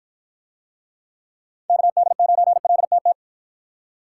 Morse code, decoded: KD9LTT

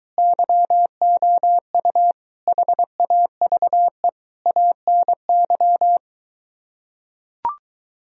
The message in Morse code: YOU HAVE ANY

